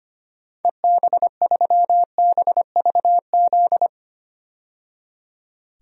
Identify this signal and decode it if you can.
Morse code, heard as a keyed tone, about 25 wpm, 700 Hz. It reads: EB3BVZ
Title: DXCC-100 Call Signs - 6 Characters - 25wpm